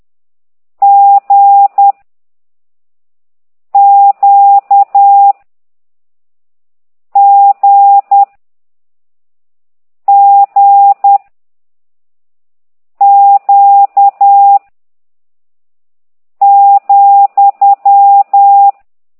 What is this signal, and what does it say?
Morse code, heard as a keyed tone, about 10 wpm, 800 Hz. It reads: G Q G G Q ,